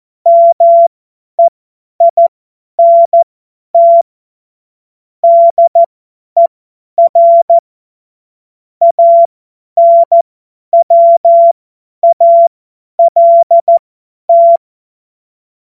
Morse code, heard as a keyed tone, about 14 wpm, spaced as for 10 wpm, 680 Hz. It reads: MEINT DER ANWALT
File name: Morsekurs / Witze / Witz 03. Mai 2022 50 F70